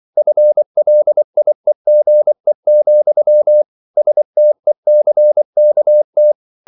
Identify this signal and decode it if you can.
Morse code, heard as a keyed tone, about 24 wpm, 600 Hz. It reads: FLIEGE, STECKT